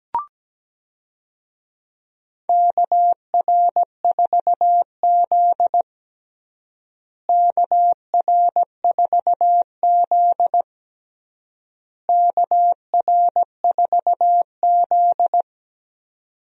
Morse code, heard as a keyed tone, about 17 wpm, 700 Hz. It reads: KR4Z KR4Z KR4Z
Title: USA Call Signs - 4 Characters - Repeated 3x - 17wpm